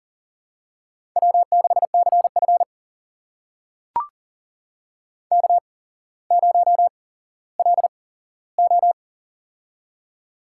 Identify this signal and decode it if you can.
Morse code, heard as a keyed tone, about 38 wpm, spaced as for 17 wpm, 700 Hz. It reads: W6CF K0LO
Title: USA Call Signs - 4 Characters - 8x Character Spacing - 40wpm